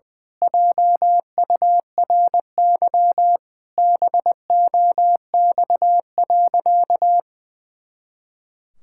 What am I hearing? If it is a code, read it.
Morse code, heard as a keyed tone, about 20 wpm, 700 Hz. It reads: JURY BOX.
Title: Pangram Challenge v2 - 20wpm